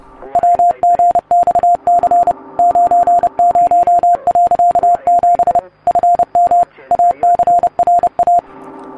Morse code, heard as a keyed tone, about 30 wpm, 700 Hz. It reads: WGXC 90.7 FM ACRA